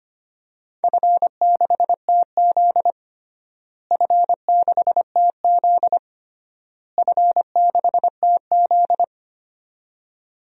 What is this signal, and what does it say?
Morse code, heard as a keyed tone, about 25 wpm, 700 Hz. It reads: F6TZ F6TZ F6TZ
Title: DXCC-100 Call Signs - 4 Characters - Repeated 3x - 25wpm